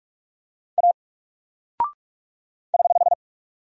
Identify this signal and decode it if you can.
Morse code, heard as a keyed tone, about 44 wpm, 700 Hz. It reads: A <HH>
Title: Single Character (Letter:Number:Punctuation:Prosign) 45wpm